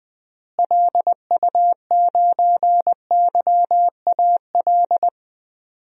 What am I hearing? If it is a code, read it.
Morse code, heard as a keyed tone, about 20 wpm, 700 Hz. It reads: LU9YAL
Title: DXCC-100 Call Signs - 6 Characters - Repeated 3x - 20wpm